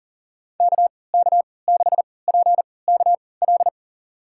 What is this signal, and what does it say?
Morse code, heard as a keyed tone, about 40 wpm, spaced as for 23 wpm, 700 Hz. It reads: KK6PKL
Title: USA Call Signs - 3x Character Spacing - 40wpm